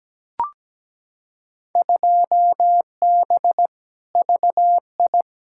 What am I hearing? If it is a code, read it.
Morse code, heard as a keyed tone, about 17 wpm, 700 Hz. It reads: 2B VI
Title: Field Day Exchanges Encore - 17wpm